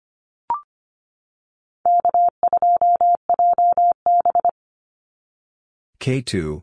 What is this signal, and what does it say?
Morse code, heard as a keyed tone, about 25 wpm, 700 Hz. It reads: K2JB
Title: NA SOTA Activator Callsigns 25wpm